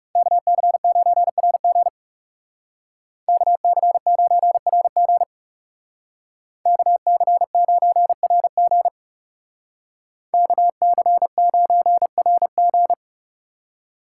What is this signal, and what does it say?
Morse code, heard as a keyed tone, about 37 wpm, 700 Hz. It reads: KC9RG KC9RG KC9RG KC9RG
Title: Speed Racing - USA Call Signs 30wpm